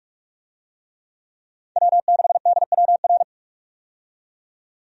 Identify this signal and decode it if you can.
Morse code, heard as a keyed tone, about 41 wpm, 700 Hz. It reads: W6DWR
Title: USA Call Signs - 5 Characters - 45wpm